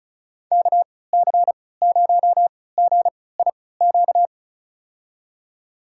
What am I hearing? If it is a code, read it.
Morse code, heard as a keyed tone, about 36 wpm, spaced as for 20 wpm, 700 Hz. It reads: KC0GIQ
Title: USA Call Signs - 6 Characters - 3x Character Spacing - 35wpm